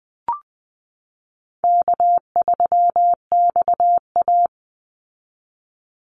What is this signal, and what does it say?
Morse code, heard as a keyed tone, about 20 wpm, 700 Hz. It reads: K3XA
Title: Easy Call Signs Repeated 3x - 20wpm